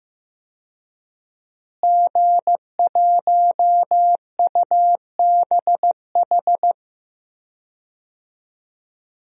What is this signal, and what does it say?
Morse code, heard as a keyed tone, about 16 wpm, 700 Hz. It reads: G1UBH
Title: DXCC-100 Call Signs - 5 Characters - Repeated 3x - 15wpm